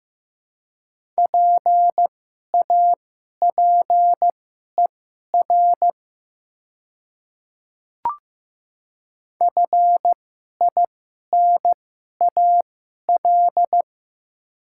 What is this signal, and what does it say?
Morse code, heard as a keyed tone, about 15 wpm, spaced as for 11 wpm, 700 Hz. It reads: PAPER FINAL
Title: 2x Character Spacing - Top 500 Words - 15wpm